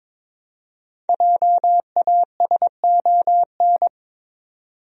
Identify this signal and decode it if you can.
Morse code, heard as a keyed tone, about 22 wpm, 700 Hz. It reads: JASON
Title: USA Common Names - Max 5 Letters - 22wpm